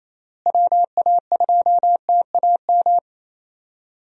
Morse code, heard as a keyed tone, about 28 wpm, 700 Hz. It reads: WA2TAM
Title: USA Call Signs - 6 Characters - Repeated 3x - 28wpm